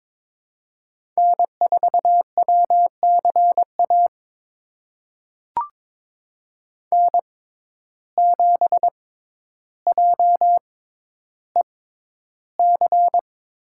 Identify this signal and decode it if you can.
Morse code, heard as a keyed tone, about 23 wpm, spaced as for 11 wpm, 700 Hz. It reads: N4WCA N7JEC
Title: USA Call Signs - 5 Characters - 6x Character Spacing - 22wpm